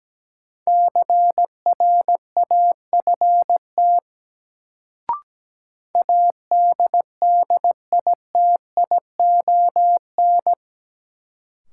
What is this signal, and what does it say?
Morse code, heard as a keyed tone, about 17 wpm, 700 Hz. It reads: CRAFT ADDITION
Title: Sentences from Top 1000 Words - Review Part 6 - New Word Drill - 17wpm